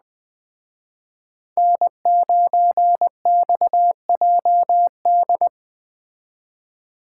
Morse code, heard as a keyed tone, about 20 wpm, 700 Hz. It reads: N9XJD